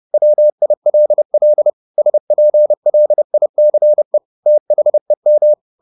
Morse code, heard as a keyed tone, about 30 wpm, 600 Hz. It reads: WILL SPLICE THEM